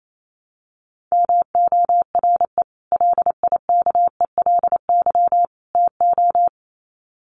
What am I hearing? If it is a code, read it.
Morse code, heard as a keyed tone, about 28 wpm, 700 Hz. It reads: MORE LIKELY TO
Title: Sets of 3 Words 28wpm